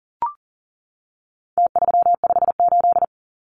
Morse code, heard as a keyed tone, about 40 wpm, 700 Hz. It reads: T358
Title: Numbers - Contest Style - 4-Digit 40wpm